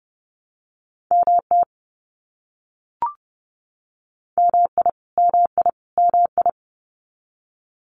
Morse code, heard as a keyed tone, about 30 wpm, 700 Hz. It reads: MT MI MI MI
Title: US States Repeated 3x 30wpm